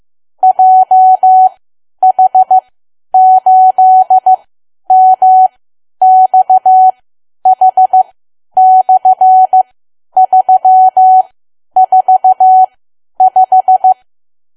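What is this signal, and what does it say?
Morse code, heard as a keyed tone, about 15 wpm, spaced as for 10 wpm, 750 Hz. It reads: JH8MXH/345
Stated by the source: Morse learning practice at 10 wpm.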